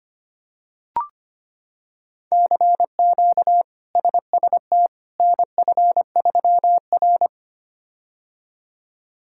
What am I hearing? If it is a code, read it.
Morse code, heard as a keyed tone, about 25 wpm, 700 Hz. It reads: CQ SST NF3R
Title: K1USN SST QSO Practice - 25wpm